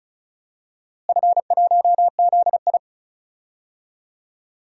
Morse code, heard as a keyed tone, about 34 wpm, 700 Hz. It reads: F1ZI
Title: DXCC-100 Call Signs - 4 Characters - Repeated 3x - 35wpm